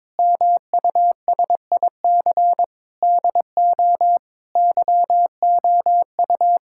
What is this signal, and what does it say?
Morse code, heard as a keyed tone, about 22 wpm, 700 Hz. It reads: MUSIC DO YOU
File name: practice / 22wpm / Sentences with 3-Word Phrases - 22wpm